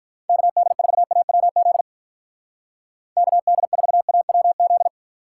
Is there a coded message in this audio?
Yes: Morse code from 0.3 to 4.9 s, heard as a keyed tone, about 44 wpm, 700 Hz. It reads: KD4AWZ KD4AWZ